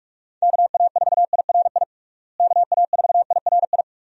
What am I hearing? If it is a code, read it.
Morse code, heard as a keyed tone, about 47 wpm, 700 Hz. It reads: KA4IRI KA4IRI
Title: USA Call Signs - 6 Characters - Repeated 2x - 45wpm